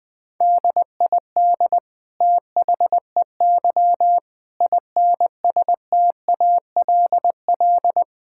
Morse code, heard as a keyed tone, about 21 wpm, 700 Hz. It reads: DID THEY INSTALL
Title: Sentences from Top 1600 Words 20wpm